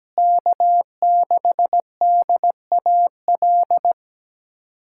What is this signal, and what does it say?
Morse code, heard as a keyed tone, about 17 wpm, 700 Hz. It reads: K6DAL
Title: USA Callsign DE Callsign - 5 Characters - 17wpm